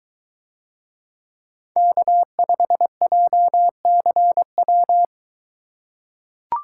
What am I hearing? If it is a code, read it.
Morse code, heard as a keyed tone, about 21 wpm, 700 Hz. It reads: K5JCW